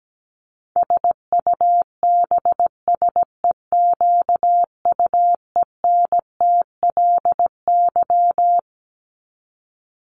Morse code, heard as a keyed tone, about 17 wpm, 700 Hz. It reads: SUBSEQUENTLY
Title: LY Suffix - Top 100 Words with Base Words 17wpm